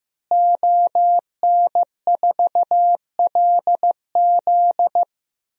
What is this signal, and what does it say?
Morse code, heard as a keyed tone, about 15 wpm, 700 Hz. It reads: ON4LZ
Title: DXCC-100 Call Signs - 5 Characters - Repeated 3x - 15wpm